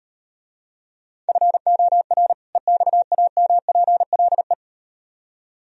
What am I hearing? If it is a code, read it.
Morse code, heard as a keyed tone, about 38 wpm, 700 Hz. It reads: FOR EXAMPLE